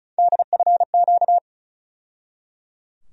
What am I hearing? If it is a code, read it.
Morse code, heard as a keyed tone, about 34 wpm, 700 Hz. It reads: DFQ
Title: USA Call Sign Postfixes - 3 Letters 35wpm